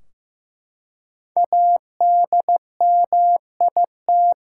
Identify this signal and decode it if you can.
Morse code, heard as a keyed tone, about 15 wpm, 700 Hz. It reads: ADMIT
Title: Five Letter Words - 15wpm